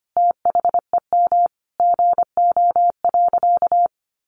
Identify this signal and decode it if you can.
Morse code, heard as a keyed tone, about 25 wpm, 700 Hz. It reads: THEM GO.
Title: Sentences from Top 800 Words 25wpm